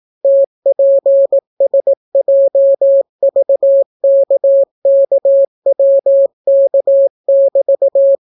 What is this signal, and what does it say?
Morse code, heard as a keyed tone, about 17 wpm, 550 Hz. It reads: TPSJVKKWK<BT>